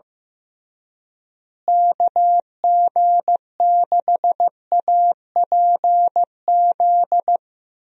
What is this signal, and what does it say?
Morse code, heard as a keyed tone, about 15 wpm, 700 Hz. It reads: KG6APZ
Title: USA Call Signs - 6 Characters - Repeated 3x - 15wpm